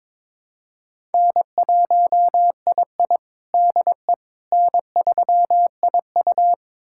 Morse code, heard as a keyed tone, about 22 wpm, 700 Hz. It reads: N1II DE N3IU
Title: USA Callsign DE Callsign - 4 Characters - 22wpm